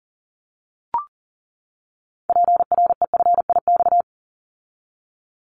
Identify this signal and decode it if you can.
Morse code, heard as a keyed tone, about 42 wpm, 700 Hz. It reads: PREFIX
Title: Callsign Prefixes for Top 25 DXCC - 40wpm